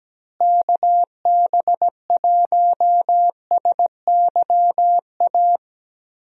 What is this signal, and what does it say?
Morse code, heard as a keyed tone, about 17 wpm, 700 Hz. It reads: KB1SYA
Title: USA Call Signs - 6 Characters - Repeated 2x - 17wpm